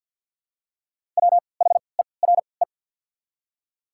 Morse code, heard as a keyed tone, about 48 wpm, spaced as for 28 wpm, 690 Hz. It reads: WHERE
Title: 3x Character Spacing - Top 300 Words - 50wpm